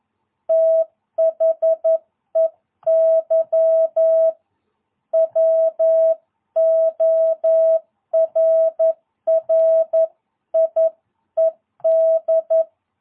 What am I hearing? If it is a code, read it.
Morse code, heard as a keyed tone, about 11 wpm, 650 Hz. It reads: THEY WORRIED
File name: jh8cbh / 2022cw